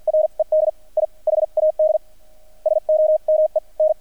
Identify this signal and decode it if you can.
Morse code, heard as a keyed tone, about 50 wpm, spaced as for 42 wpm, 620 Hz. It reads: RED I HAD SOME N
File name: archived / CRSnet